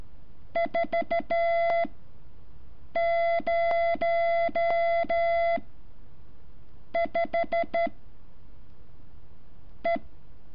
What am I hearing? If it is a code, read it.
Morse code, heard as a keyed tone, about 13 wpm, spaced as for 6 wpm, 680 Hz. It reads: 40 5 E
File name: curso / ejercicio 14